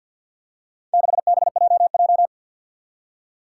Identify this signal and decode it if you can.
Morse code, heard as a keyed tone, about 48 wpm, 700 Hz. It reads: 6BJJ
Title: Four Challenging Letter-Number Combinations 50wpm